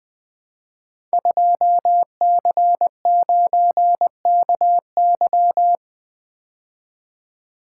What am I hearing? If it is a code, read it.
Morse code, heard as a keyed tone, about 20 wpm, 700 Hz. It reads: 2C9KY
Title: Code Groups - 5 Letter:Numbers - 20wpm